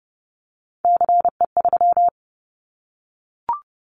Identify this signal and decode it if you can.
Morse code, heard as a keyed tone, about 30 wpm, 700 Hz. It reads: CE3
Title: Callsign Prefixes for Top 100 DXCC - 30wpm